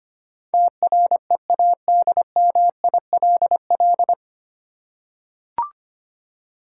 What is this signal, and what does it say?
Morse code, heard as a keyed tone, about 25 wpm, 690 Hz. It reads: TREADMILL
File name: practice / 25wpm / Easy Words with Hints - 25wpm